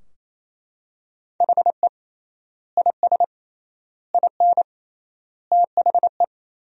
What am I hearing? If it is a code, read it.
Morse code, heard as a keyed tone, about 28 wpm, 700 Hz. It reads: HE IS IN THE
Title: 3x Word-Spacing - Sentences from Top 500 Words - 28wpm